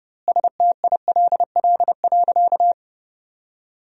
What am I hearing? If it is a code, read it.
Morse code, heard as a keyed tone, about 32 wpm, 700 Hz. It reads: STILL.
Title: 2x Word-Spacing - Sentences from Top 400 Words - 30wpm